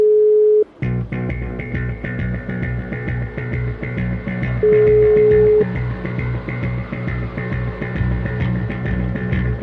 Ringback tone (Europe, 425 Hz).